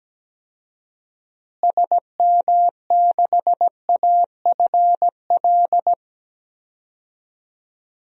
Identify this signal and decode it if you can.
Morse code, heard as a keyed tone, about 17 wpm, 700 Hz. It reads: SM6AFL
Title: DXCC-100 Call Signs - 6 Characters - Repeated 3x - 17wpm